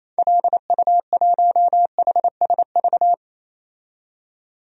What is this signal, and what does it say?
Morse code, heard as a keyed tone, about 28 wpm, 700 Hz. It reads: LU1HSV
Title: DXCC-100 Call Signs - 6 Characters - 28wpm